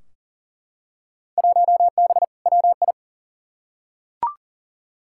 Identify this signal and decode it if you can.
Morse code, heard as a keyed tone, about 38 wpm, 700 Hz. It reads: 1B WI
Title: Field Day Exchanges - 40wpm